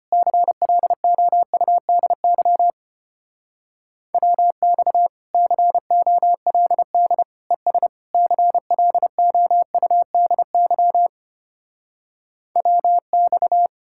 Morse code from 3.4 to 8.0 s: WX COLD ES

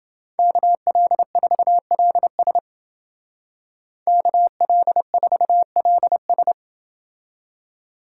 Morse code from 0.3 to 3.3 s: KL4LS

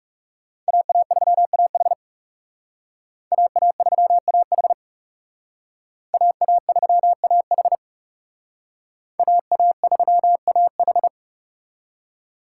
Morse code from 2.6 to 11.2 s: AA3AH AA3AH AA3AH

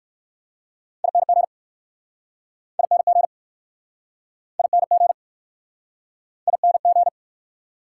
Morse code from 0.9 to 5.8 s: ING ING ING

